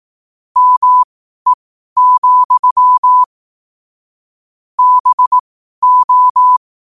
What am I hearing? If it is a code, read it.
Morse code, heard as a keyed tone, about 18 wpm, spaced as for 12 wpm, 1000 Hz. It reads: ME, BO
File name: qotdfiles / 12wpm